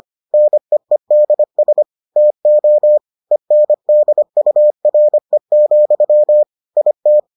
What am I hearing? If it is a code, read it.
Morse code, heard as a keyed tone, about 25 wpm, 600 Hz. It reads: NEEDS TO ENDURE, IT